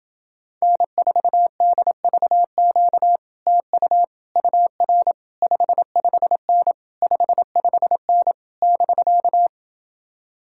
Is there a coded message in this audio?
Yes: Morse code from 0.6 to 9.5 s, heard as a keyed tone, about 28 wpm, 700 Hz. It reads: N4DVQ TU UR 55N 55N <BK>